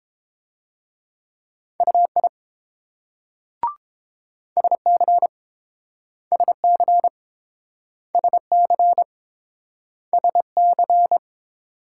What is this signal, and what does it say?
Morse code, heard as a keyed tone, about 30 wpm, 700 Hz. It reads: UI SC SC SC SC